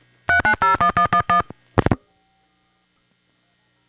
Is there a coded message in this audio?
Yes: DTMF from 0.3 to 1.4 s, heard as telephone touch tones, tones of about 100 ms. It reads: 3901111